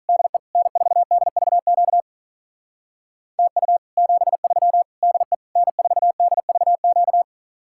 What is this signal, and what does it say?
Morse code, heard as a keyed tone, about 46 wpm, 700 Hz. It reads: DE N4DVQ TU 73 DE N4DVQ